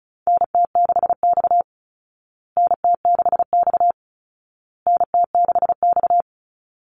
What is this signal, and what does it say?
Morse code, heard as a keyed tone, about 34 wpm, 700 Hz. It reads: NT6X NT6X NT6X
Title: Call Signs Repeated 2x - 35wpm